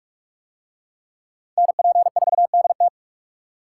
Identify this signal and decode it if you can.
Morse code, heard as a keyed tone, about 44 wpm, 700 Hz. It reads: NP4DT